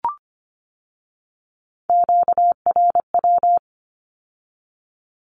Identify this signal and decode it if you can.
Morse code, heard as a keyed tone, about 25 wpm, 700 Hz. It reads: QRW
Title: Q-Codes with Spoken Meaning - 25wpm